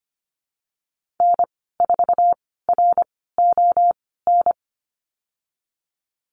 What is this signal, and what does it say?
Morse code, heard as a keyed tone, about 25 wpm, spaced as for 16 wpm, 700 Hz. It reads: N4RON
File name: Lesson 15g - Number 5 - Call Signs 25wpm